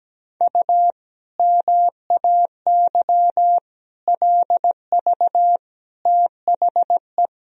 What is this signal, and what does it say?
Morse code, heard as a keyed tone, about 17 wpm, 700 Hz. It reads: U MAY LV THE
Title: Sentences from Top 400 Words and Abbreviations 17wpm